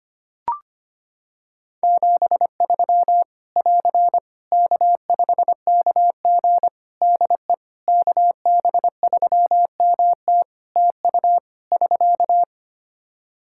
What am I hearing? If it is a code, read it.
Morse code, heard as a keyed tone, about 25 wpm, 700 Hz. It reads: 73 <AR> K5KG DE KB3MT TU <SK>
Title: K1USN SST QSO Practice 25wpm